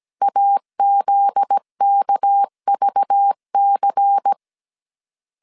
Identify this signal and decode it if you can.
Morse code, heard as a keyed tone, about 17 wpm, 800 Hz. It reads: AZKVC
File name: CW / DET / 1 9 1500C S C 16wpm 2909